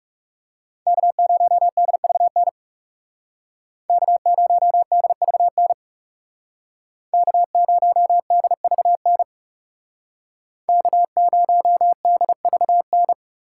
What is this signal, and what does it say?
Morse code, heard as a keyed tone, about 39 wpm, 700 Hz. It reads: K0DVN K0DVN K0DVN K0DVN